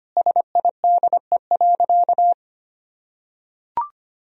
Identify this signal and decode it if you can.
Morse code, heard as a keyed tone, about 25 wpm, 700 Hz. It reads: SIDE.
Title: Sentences from Top 1000 Words - Review Part 1 - 25wpm